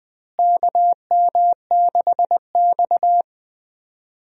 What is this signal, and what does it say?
Morse code, heard as a keyed tone, about 20 wpm, 700 Hz. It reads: KM6X
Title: USA Call Signs - 4 Characters - Repeated 3x - 20wpm